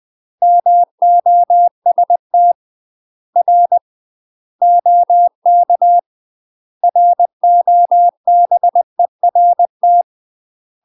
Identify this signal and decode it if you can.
Morse code, heard as a keyed tone, about 20 wpm, spaced as for 17 wpm, 700 Hz. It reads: MOSTR OK ROBERT
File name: UK MASTER QSOs TRAINING / 20WPM / UK MASTER QSOs 0043